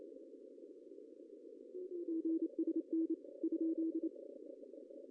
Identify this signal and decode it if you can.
Morse code, heard as a keyed tone, about 27 wpm, 310 Hz. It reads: GSN ?
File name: CW / cqww-cw-k5tr